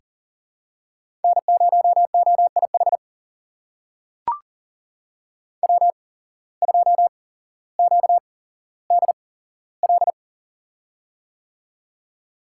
Morse code, heard as a keyed tone, about 39 wpm, spaced as for 17 wpm, 700 Hz. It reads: N0OIH W2QDL
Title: USA Call Signs - 5 Characters - 8x Character Spacing - 40wpm